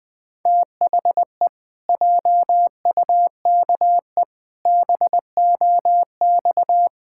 Morse code, heard as a keyed tone, about 20 wpm, 700 Hz. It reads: THE JUKE BOX
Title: Pangram Challenge v2 - 20wpm